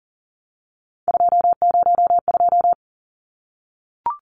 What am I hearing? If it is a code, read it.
Morse code, heard as a keyed tone, about 40 wpm, 700 Hz. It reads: 202
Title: Numbers - 3-Digit 40wpm